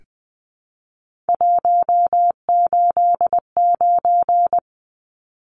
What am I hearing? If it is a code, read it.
Morse code, heard as a keyed tone, about 20 wpm, 700 Hz. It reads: 189
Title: Numbers - 3-Digit 20wpm